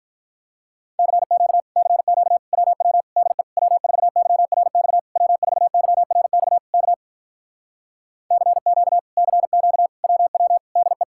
Morse code, heard as a keyed tone, about 47 wpm, 700 Hz. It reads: CQ CQ WW DE W4YRX W4YRX K CQ CQ WW DE